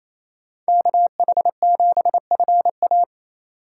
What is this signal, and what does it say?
Morse code, heard as a keyed tone, about 27 wpm, 700 Hz. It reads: KH7FA
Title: USA Call Signs - 5 Characters - 2x Character Spacing - 28wpm